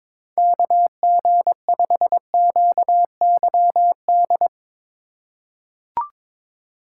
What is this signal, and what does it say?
Morse code, heard as a keyed tone, about 22 wpm, 700 Hz. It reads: KG5QYD
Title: USA Call Signs - 6 Characters - 3x Character Spacing - 22wpm